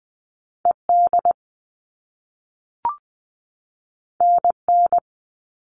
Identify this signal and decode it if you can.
Morse code, heard as a keyed tone, about 20 wpm, 700 Hz. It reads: ED NN
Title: All Cut Numbers - 2-Digit 20wpm